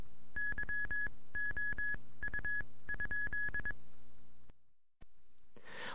Morse code, heard as a keyed tone, about 22 wpm, spaced as for 17 wpm, 1700 Hz. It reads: YOU?